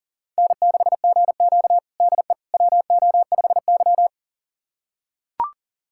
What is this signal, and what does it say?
Morse code, heard as a keyed tone, about 40 wpm, 700 Hz. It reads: N6GQ DE WO5Y
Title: USA Callsign DE Callsign - 4 Characters - 40wpm